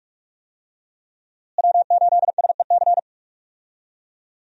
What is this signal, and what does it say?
Morse code, heard as a keyed tone, about 46 wpm, 700 Hz. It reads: W8SEC
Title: USA Call Signs - 5 Characters - Repeated 2x - 45wpm